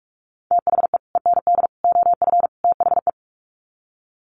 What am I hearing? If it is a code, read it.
Morse code, heard as a keyed tone, about 44 wpm, 700 Hz. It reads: THE END OF THE
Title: Sets of 4 Words 45wpm